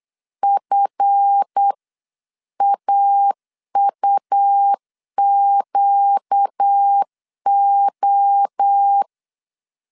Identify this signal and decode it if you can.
Morse code, heard as a keyed tone, about 8 wpm, 800 Hz. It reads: FAUQO
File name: CW / DET / 1 9 1500C S B 8wpm 5624